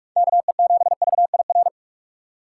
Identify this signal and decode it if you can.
Morse code, heard as a keyed tone, about 46 wpm, 700 Hz. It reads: KE7VIR